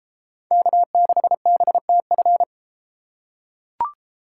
Morse code, heard as a keyed tone, about 31 wpm, 700 Hz. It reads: K6BTF